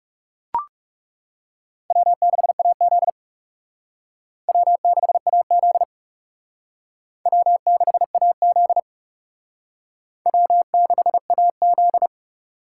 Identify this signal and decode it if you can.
Morse code, heard as a keyed tone, about 35 wpm, 700 Hz. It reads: W6AZ W6AZ W6AZ W6AZ
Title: Speed Racing - USA 1x2 Call Signs 30wpm